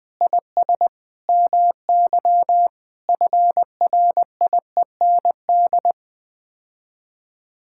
Morse code, heard as a keyed tone, about 21 wpm, 700 Hz. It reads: IS MY FRIEND
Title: Sets of 8 Words - 20wpm